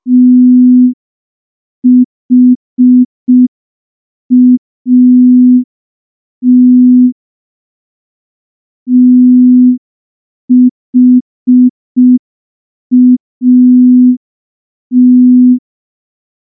Morse code, heard as a keyed tone, about 5 wpm, 250 Hz. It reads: THAT THAT
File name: morse / morse-training-05WPM